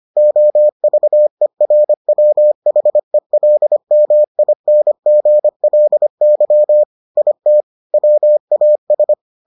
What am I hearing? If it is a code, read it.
Morse code, heard as a keyed tone, about 25 wpm, 600 Hz. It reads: OVERWHELMINGLY IT WAS